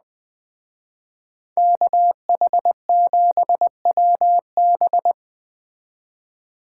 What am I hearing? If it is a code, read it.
Morse code, heard as a keyed tone, about 20 wpm, 700 Hz. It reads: KH7WB
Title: Speed Racing - USA Call Signs 17wpm